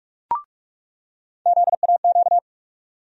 Morse code, heard as a keyed tone, about 45 wpm, 700 Hz. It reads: ZAQ
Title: USA Call Sign Postfixes - 3 Letters 45wpm